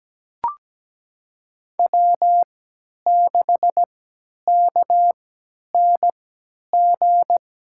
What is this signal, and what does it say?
Morse code, heard as a keyed tone, about 17 wpm, spaced as for 10 wpm, 700 Hz. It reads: W6KNG
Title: USA Call Signs - 3x Character Spacing - 17wpm